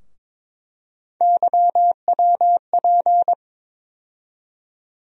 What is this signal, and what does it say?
Morse code, heard as a keyed tone, about 22 wpm, 700 Hz. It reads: YWP
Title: USA Call Sign Postfixes - 3 Letters 22wpm